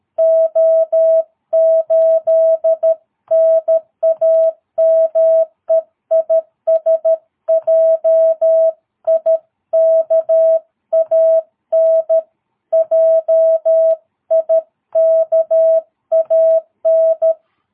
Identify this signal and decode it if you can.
Morse code, heard as a keyed tone, about 13 wpm, 650 Hz. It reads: O8 NAME IS JIKAN JIKAN
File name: jh8cbh / 2022cw